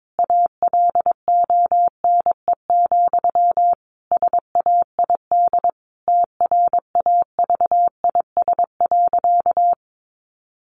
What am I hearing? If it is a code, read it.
Morse code, heard as a keyed tone, about 22 wpm, 700 Hz. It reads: ALONE, SAID TRAVIS.
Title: A Sound Of Thunder 22wpm